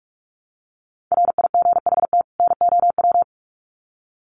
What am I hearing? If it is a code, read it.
Morse code, heard as a keyed tone, about 44 wpm, 700 Hz. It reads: RIGHT NOW
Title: Sets of 2 Words 45wpm